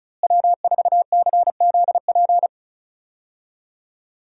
Morse code, heard as a keyed tone, about 36 wpm, 700 Hz. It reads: W4CZP
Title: USA Call Signs - 5 Characters - Repeated 3x - 35wpm